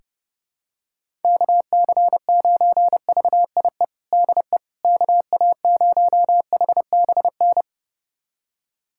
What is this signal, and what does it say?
Morse code, heard as a keyed tone, about 30 wpm, 700 Hz. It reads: KC9VIE DE KA0HBN
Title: USA Callsign DE Callsign - 6 Characters - 30wpm